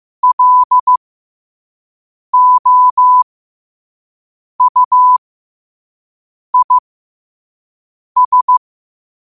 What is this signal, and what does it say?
Morse code, heard as a keyed tone, about 15 wpm, spaced as for 5 wpm, 1000 Hz. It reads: LOUIS
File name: qotdfiles / 5wpm